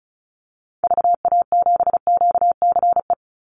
Morse code, heard as a keyed tone, about 34 wpm, 700 Hz. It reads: VA7QCE
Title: Call Signs Repeated 3x - 35wpm